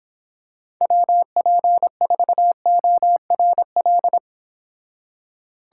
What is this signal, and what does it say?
Morse code, heard as a keyed tone, about 26 wpm, 700 Hz. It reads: WP4ORL